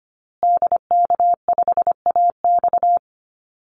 Morse code, heard as a keyed tone, about 25 wpm, 700 Hz. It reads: DK5AX
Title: Call Signs Repeated 2x - 25wpm